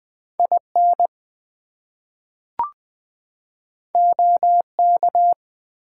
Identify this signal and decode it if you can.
Morse code, heard as a keyed tone, about 20 wpm, 700 Hz. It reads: IN OK
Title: Two Letter Words - 20wpm